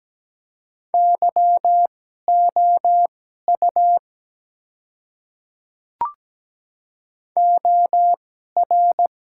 Morse code, heard as a keyed tone, about 17 wpm, spaced as for 12 wpm, 700 Hz. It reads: YOU OR
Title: 2x Character Spacing - Top 100 Words - 17wpm